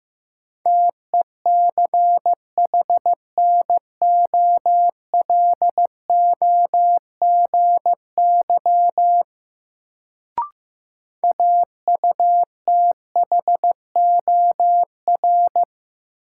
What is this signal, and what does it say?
Morse code, heard as a keyed tone, about 15 wpm, 700 Hz. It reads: TECHNOLOGY AUTHOR
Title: Sentences from Top 1200 Words - New Word Drill Alt - 15wpm